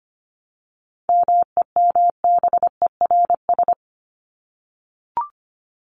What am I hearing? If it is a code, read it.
Morse code, heard as a keyed tone, about 25 wpm, 700 Hz. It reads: MEMBERS
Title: ES:S Suffix - Top 100 Words with Base Words 25wpm